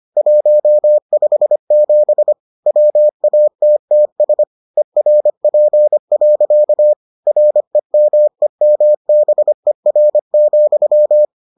Morse code, heard as a keyed tone, about 25 wpm, 600 Hz. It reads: 157 WATTS ERP. REMEMBER,